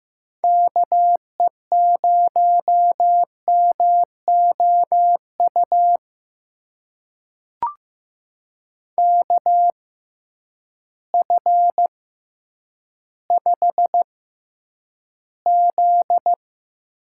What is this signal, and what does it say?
Morse code, heard as a keyed tone, about 15 wpm, spaced as for 9 wpm, 700 Hz. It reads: KE0MOU KF5Z
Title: USA Call Signs - 6 Characters - 6x Character Spacing - 15wpm